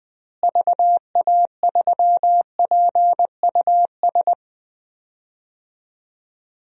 Morse code, heard as a keyed tone, about 20 wpm, 700 Hz. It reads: VA3PUS
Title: DXCC-100 Call Signs - 6 Characters - Repeated 2x - 20wpm